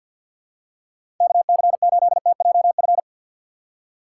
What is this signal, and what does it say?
Morse code, heard as a keyed tone, about 53 wpm, 700 Hz. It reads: KC8TJF
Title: USA Call Signs - 6 Characters - 3x Character Spacing - 50wpm